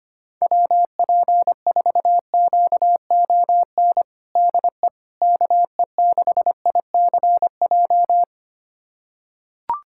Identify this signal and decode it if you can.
Morse code, heard as a keyed tone, about 25 wpm, 700 Hz. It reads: WP4QON DE KE6ICJ
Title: USA Callsign DE Callsign 25wpm